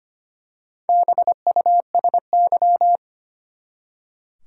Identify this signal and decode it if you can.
Morse code, heard as a keyed tone, about 25 wpm, 700 Hz. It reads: BUSY
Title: Four Letter Words - 25wpm